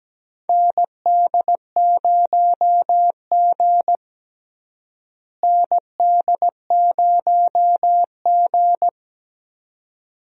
Morse code, heard as a keyed tone, about 17 wpm, 700 Hz. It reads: ND0G ND0G
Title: USA Call Signs - 4 Characters - Repeated 2x - 17wpm